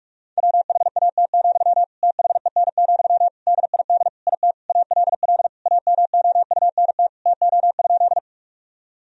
Morse code, heard as a keyed tone, about 41 wpm, 700 Hz. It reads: WHAT, THEN, DID IT ALL AMOUNT TO?
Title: Sentences from Top 1000 Words - Review Part 4 - 45wpm